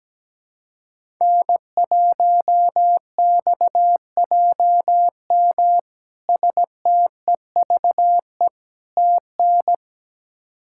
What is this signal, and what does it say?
Morse code, heard as a keyed tone, about 17 wpm, 700 Hz. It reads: N1XJM STEVE TN
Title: K1USN SST QSO Practice - 17wpm